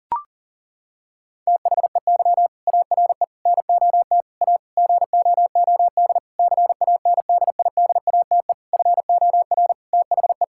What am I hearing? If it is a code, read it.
Morse code, heard as a keyed tone, about 40 wpm, 700 Hz. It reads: THEY ARE NOT A GOOD CANDIDATE FOR THE
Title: Sentences from Top 1600 Words 40wpm